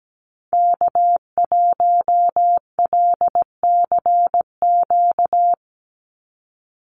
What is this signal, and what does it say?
Morse code, heard as a keyed tone, about 17 wpm, 700 Hz. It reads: K1LCQ
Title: USA Call Signs 17wpm